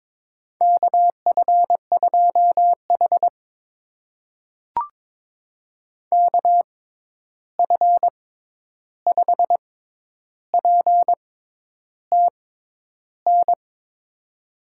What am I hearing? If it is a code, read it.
Morse code, heard as a keyed tone, about 22 wpm, spaced as for 10 wpm, 700 Hz. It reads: KF2H KF5PTN
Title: USA Call Signs - 6x Character Spacing - 22wpm